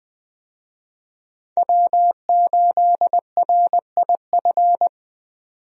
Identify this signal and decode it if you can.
Morse code, heard as a keyed tone, about 20 wpm, 700 Hz. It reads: W8RIF
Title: USA Call Signs - 5 Characters - 20wpm